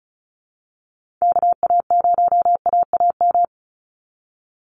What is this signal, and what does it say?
Morse code, heard as a keyed tone, about 34 wpm, 700 Hz. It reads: KA0AAM
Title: Call Signs Repeated 3x - 35wpm